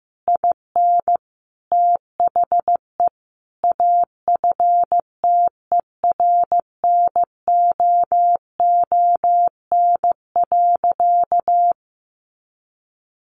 Morse code, heard as a keyed tone, about 15 wpm, 700 Hz. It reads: IN THE AFTERNOON.
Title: Sentences from Top 1500 Words 15wpm